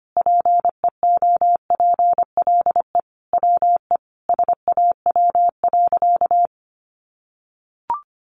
Morse code, heard as a keyed tone, about 25 wpm, 700 Hz. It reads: PEOPLE WE SAW.
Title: Sentences from Top 300 Words 25wpm